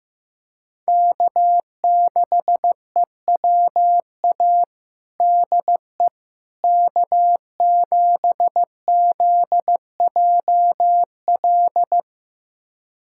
Morse code, heard as a keyed tone, about 14 wpm, 700 Hz. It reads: K6EWA DE K7ZJL
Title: USA Callsign DE Callsign - 5 Characters - 15wpm